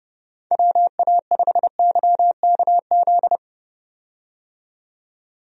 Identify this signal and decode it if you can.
Morse code, heard as a keyed tone, about 32 wpm, 700 Hz. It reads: WA5YKZ
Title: USA Call Signs - 6 Characters - 30wpm